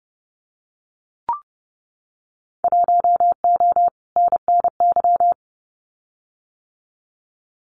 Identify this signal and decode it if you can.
Morse code, heard as a keyed tone, about 30 wpm, 700 Hz. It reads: 1O NNY
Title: Winter Field Day Exchanges 30wpm